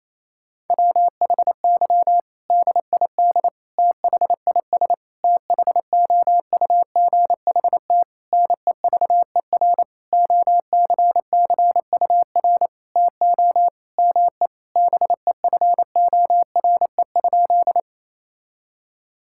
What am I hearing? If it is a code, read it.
Morse code, heard as a keyed tone, about 27 wpm, 700 Hz. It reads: WHY DID THIS THOUGHT NEVER OCCUR TO ME BEFORE?
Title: Sentences from Top 1000 Words - Review Part 6 - 28wpm